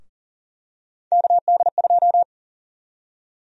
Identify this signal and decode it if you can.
Morse code, heard as a keyed tone, about 40 wpm, 700 Hz. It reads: KD2
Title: USA Call Sign Prefixes - Repeated 2x - 40wpm